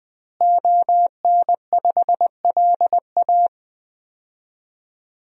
Morse code, heard as a keyed tone, about 20 wpm, 700 Hz. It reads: ON5LA
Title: DXCC-100 Call Signs - 5 Characters - Repeated 3x - 20wpm